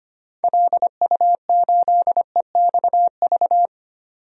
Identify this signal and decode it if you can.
Morse code, heard as a keyed tone, about 25 wpm, 700 Hz. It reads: LU8EXV
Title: DXCC-100 Call Signs - 6 Characters - 25wpm